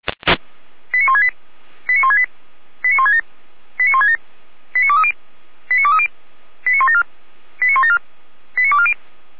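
Five-tone selective call (ZVEI), heard as quick runs of five tones, tones of about 70 ms. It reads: 89168 89168 89167 89167 89230 89230 89174 89174 89240
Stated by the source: ZVEI five-tone call 89140 received by SDR, from the Feuerwehr Orlamünde audio server.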